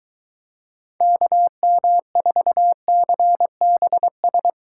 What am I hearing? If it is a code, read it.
Morse code, heard as a keyed tone, about 23 wpm, 700 Hz. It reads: KM4CBS